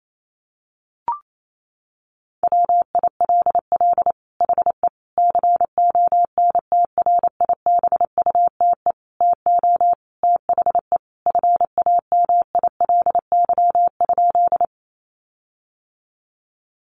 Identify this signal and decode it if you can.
Morse code, heard as a keyed tone, about 28 wpm, 700 Hz. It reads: WILL HE CONTRIBUTE TO THE FAMILY?
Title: Sentences from Top 1400 Words 28wpm